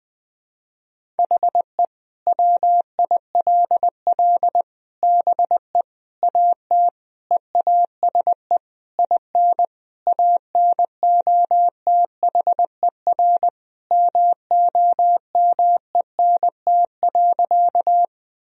Morse code, heard as a keyed tone, about 20 wpm, 700 Hz. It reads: HE WILL BE AT EASE IN ANOTHER MOMENT.
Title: Sentences from Top 1000 Words - Review Part 1 - 20wpm